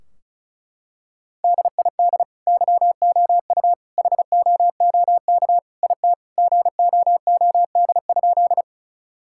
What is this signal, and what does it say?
Morse code, heard as a keyed tone, about 36 wpm, 700 Hz. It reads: DID YOU HOOK IT GOOD?
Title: Sentences from Top 1600 Words 35wpm